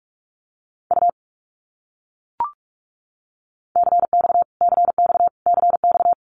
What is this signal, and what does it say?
Morse code, heard as a keyed tone, about 44 wpm, 700 Hz. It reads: U CX CX CX
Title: Abbreviations Repeated 3x - 45wpm